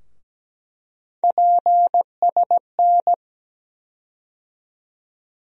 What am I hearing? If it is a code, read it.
Morse code, heard as a keyed tone, about 17 wpm, 700 Hz. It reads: PSN
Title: USA Call Sign Postfixes - 3 Letters 17wpm